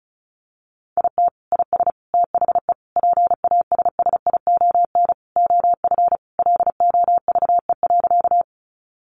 Morse code, heard as a keyed tone, about 34 wpm, 700 Hz. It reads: IT IS THE PASSION OF LOVE.
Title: Sentences from Top 1300 Words 35wpm